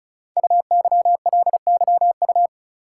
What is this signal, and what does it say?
Morse code, heard as a keyed tone, about 36 wpm, 700 Hz. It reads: UYLYU
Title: Code Groups - 5 Letters - 35wpm